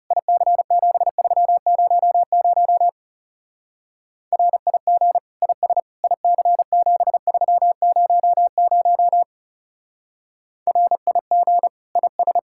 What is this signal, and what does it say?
Morse code, heard as a keyed tone, about 36 wpm, 700 Hz. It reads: IC7300 RIG IS IC7300 RIG IS
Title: Speed-Racing Common QSO Phrases Encore - 30wpm